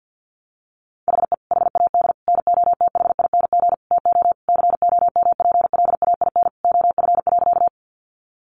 Morse code, heard as a keyed tone, about 50 wpm, 700 Hz. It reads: HE HAD NOTHING TO COMPLAIN OF.
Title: Sentences from Top 1300 Words 50wpm